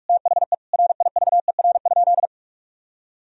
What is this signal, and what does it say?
Morse code, heard as a keyed tone, about 46 wpm, 700 Hz. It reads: THE RIVER?
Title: Sentences from Top 1000 Words - Review Part 4 - 45wpm